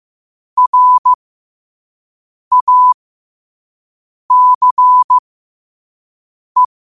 Morse code, heard as a keyed tone, about 15 wpm, spaced as for 5 wpm, 1000 Hz. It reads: RACE
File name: qotdfiles / 5wpm